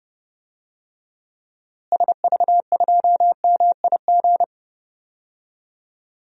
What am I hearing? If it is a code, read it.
Morse code, heard as a keyed tone, about 30 wpm, 700 Hz. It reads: SV2MIG
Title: DXCC-100 Call Signs - 6 Characters - Repeated 3x - 30wpm